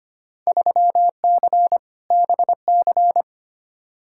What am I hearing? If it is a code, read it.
Morse code, heard as a keyed tone, about 25 wpm, 700 Hz. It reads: 3C BC
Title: Field Day Exchanges Encore - 25wpm